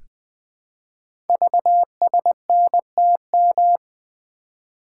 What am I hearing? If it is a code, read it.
Morse code, heard as a keyed tone, about 20 wpm, 700 Hz. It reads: VSNTM
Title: Code Groups - 5 Letter:Numbers - 20wpm